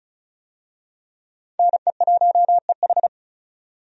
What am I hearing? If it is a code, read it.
Morse code, heard as a keyed tone, about 34 wpm, 700 Hz. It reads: NE1EH
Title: USA Call Signs - 5 Characters - 35wpm